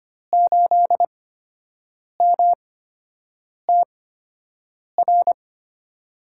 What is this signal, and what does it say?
Morse code, heard as a keyed tone, about 25 wpm, spaced as for 7 wpm, 700 Hz. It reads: 8MTR
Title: USA Call Signs - 8x Character Spacing - 25wpm